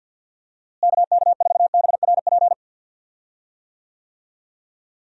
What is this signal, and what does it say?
Morse code, heard as a keyed tone, about 49 wpm, 700 Hz. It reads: KK4BRP
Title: USA Call Signs - 6 Characters - Repeated 2x - 50wpm